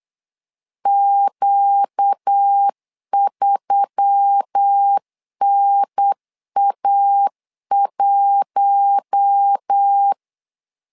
Morse code, heard as a keyed tone, about 8 wpm, 800 Hz. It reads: Q3NA1
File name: LEZIONI / 5 LBVC / 5 L 8wpm 2831